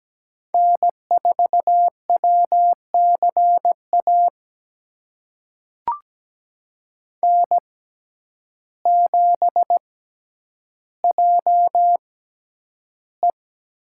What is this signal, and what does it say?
Morse code, heard as a keyed tone, about 17 wpm, spaced as for 9 wpm, 700 Hz. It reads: N4WCA N7JE
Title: USA Call Signs - 5 Characters - 6x Character Spacing - 17wpm